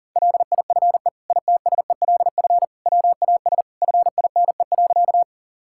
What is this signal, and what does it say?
Morse code, heard as a keyed tone, about 40 wpm, 700 Hz. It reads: LIFE ITSELF WAS FINE.
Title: Sentences from Top 1000 Words - Review Part 5 - 40wpm